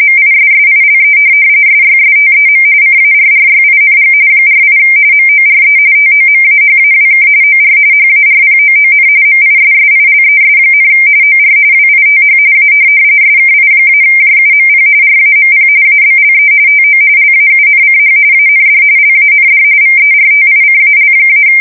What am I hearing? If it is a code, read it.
RTTY, about 45.45 baud. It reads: MISSILE SYSTEM CARRYING 48 MORE TOMAHAWK OR HARPOON MISSILES, OR WITH A SKI-JUMP RAMP ABLE TO LAUNCH AND LAND VERTICAL-
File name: fsk170 / USS-New-Jersey